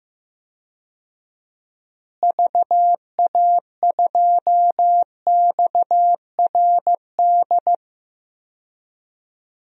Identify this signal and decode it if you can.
Morse code, heard as a keyed tone, about 15 wpm, 700 Hz. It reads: VA2XRD
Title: DXCC-100 Call Signs - 6 Characters - Repeated 3x - 15wpm